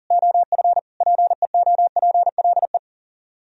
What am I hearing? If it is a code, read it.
Morse code, heard as a keyed tone, about 52 wpm, 700 Hz. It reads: OF PEOPLE